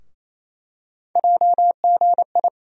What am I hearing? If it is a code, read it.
Morse code, heard as a keyed tone, about 28 wpm, 700 Hz. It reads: JGI
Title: USA Call Sign Postfixes - 3 Letters 28wpm